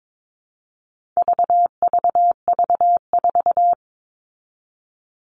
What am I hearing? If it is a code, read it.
Morse code, heard as a keyed tone, about 22 wpm, 700 Hz. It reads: VVV4
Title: All Cut Numbers - 4-Digit 22wpm